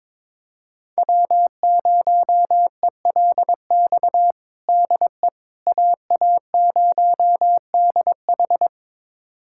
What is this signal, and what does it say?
Morse code, heard as a keyed tone, about 22 wpm, 700 Hz. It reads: W0ELX DE AA0DH
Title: USA Callsign DE Callsign - 5 Characters - 22wpm